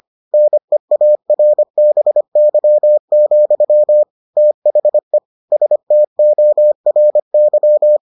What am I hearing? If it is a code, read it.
Morse code, heard as a keyed tone, about 25 wpm, 600 Hz. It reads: NEARBY, THE STORY